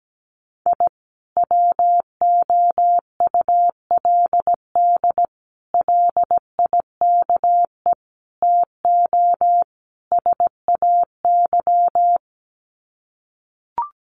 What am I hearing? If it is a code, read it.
Morse code, heard as a keyed tone, about 17 wpm, 700 Hz. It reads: I WOULD LIKE TO SAY
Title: Sets of 5 Words Encore2 17wpm